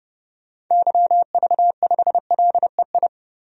Morse code, heard as a keyed tone, about 32 wpm, 700 Hz. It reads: YV5LEI
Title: DXCC-100 Call Signs - 6 Characters - 30wpm